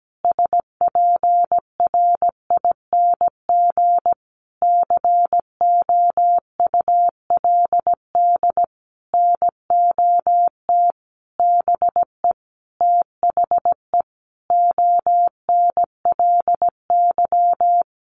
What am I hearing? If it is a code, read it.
Morse code, heard as a keyed tone, about 17 wpm, 700 Hz. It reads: SPRING COULD NOT BE THE ONLY
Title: Sentences from Top 1500 Words 17wpm